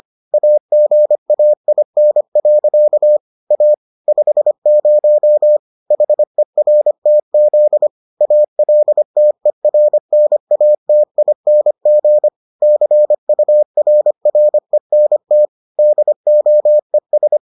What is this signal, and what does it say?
Morse code, heard as a keyed tone, about 25 wpm, 600 Hz. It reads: AGAIN. A 50 HERTZ ALTERNATING CURRENT DOES